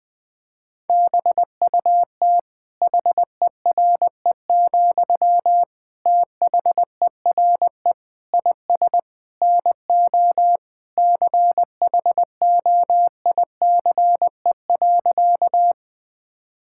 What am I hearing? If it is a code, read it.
Morse code, heard as a keyed tone, about 20 wpm, 700 Hz. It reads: BUT HERE, THERE IS NO CHOICE.
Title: Sentences from Top 1000 Words - Review Part 6 - 20wpm